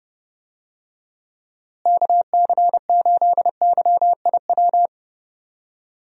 Morse code, heard as a keyed tone, about 31 wpm, 700 Hz. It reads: KC8YIW
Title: USA Call Signs - 6 Characters - Repeated 3x - 30wpm